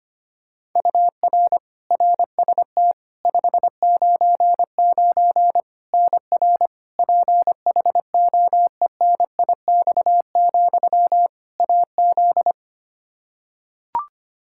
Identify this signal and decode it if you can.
Morse code, heard as a keyed tone, about 25 wpm, 700 Hz. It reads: UR RST 599 NR PHOENIX, AZ
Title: Practice QSO - Exchange 1 - Beginning - 25wpm